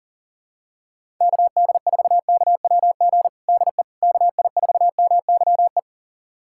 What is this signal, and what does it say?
Morse code, heard as a keyed tone, about 40 wpm, 700 Hz. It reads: KD4KWG DE KI4MYE
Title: USA Callsign DE Callsign 40wpm